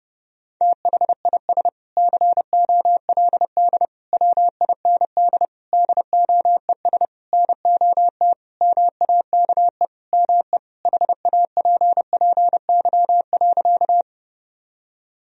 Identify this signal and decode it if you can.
Morse code, heard as a keyed tone, about 32 wpm, 700 Hz. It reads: THIS COLD WIND DOES NOT MAKE ME HAPPY.
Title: Sentences from Top 1000 Words - Review Part 2 - 30wpm